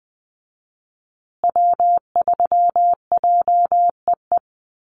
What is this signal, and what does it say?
Morse code, heard as a keyed tone, about 20 wpm, 700 Hz. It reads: W3JEE
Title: USA Call Signs Repeated 2x 20wpm